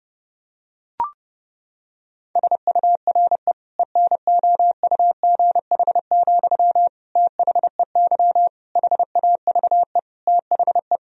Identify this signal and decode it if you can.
Morse code, heard as a keyed tone, about 30 wpm, 700 Hz. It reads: SURE ENOUGH, THEY HAVE THE
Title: Sentences from Top 1000 Words - Review Part 2 - 30wpm